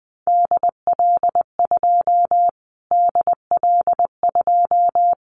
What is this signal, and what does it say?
Morse code, heard as a keyed tone, about 19 wpm, 700 Hz. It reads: DL2 DL2
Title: Callsign Prefixes for Top 25 DXCC - Repeated 3x - 20wpm